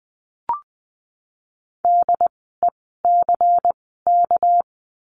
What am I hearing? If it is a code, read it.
Morse code, heard as a keyed tone, about 20 wpm, spaced as for 15 wpm, 700 Hz. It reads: DECK
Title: 199 Lesson 32f - Letter k - Words 20wpm